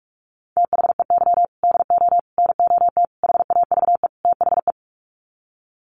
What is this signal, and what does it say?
Morse code, heard as a keyed tone, about 44 wpm, 700 Hz. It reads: THEY DO NOT HAVE THE
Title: Sets of 5 Words 45wpm